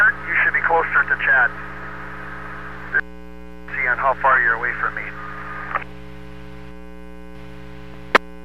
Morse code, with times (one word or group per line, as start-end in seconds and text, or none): none